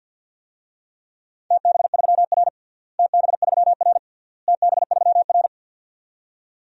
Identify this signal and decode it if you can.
Morse code, heard as a keyed tone, about 49 wpm, 700 Hz. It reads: TB3R TB3R TB3R
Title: DXCC-100 Call Signs - 4 Characters - Repeated 3x - 50wpm